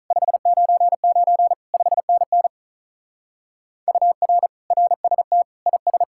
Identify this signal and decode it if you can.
Morse code, heard as a keyed tone, about 43 wpm, 700 Hz. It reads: 599 5NN UR RST IS
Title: Speed-Racing Common QSO Phrases Encore - 35wpm